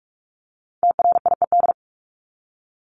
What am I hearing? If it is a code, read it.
Morse code, heard as a keyed tone, about 44 wpm, 700 Hz. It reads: TRIED
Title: ED Suffix - Top 100 Words with Base Words 45wpm